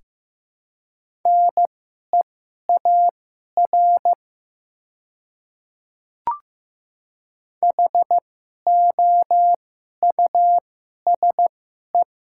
Morse code, heard as a keyed tone, about 15 wpm, spaced as for 11 wpm, 700 Hz. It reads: NEAR HOUSE
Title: 2x Character Spacing - Top 200 Words - 15wpm